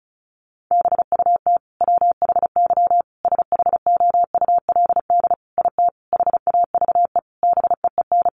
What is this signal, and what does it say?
Morse code, heard as a keyed tone, about 34 wpm, 700 Hz. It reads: BUT WHY SHOULD IT HAVE BEEN
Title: Sentences from Top 1200 Words 35wpm